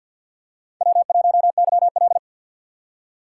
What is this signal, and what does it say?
Morse code, heard as a keyed tone, about 53 wpm, 700 Hz. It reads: W1YL
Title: USA Call Signs - 3x Character Spacing - 50wpm